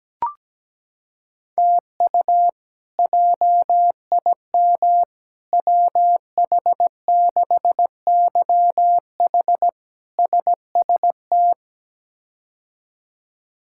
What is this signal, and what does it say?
Morse code, heard as a keyed tone, about 17 wpm, 700 Hz. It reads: TU JIM WH6YH SST
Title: K1USN SST QSO Practice - 17wpm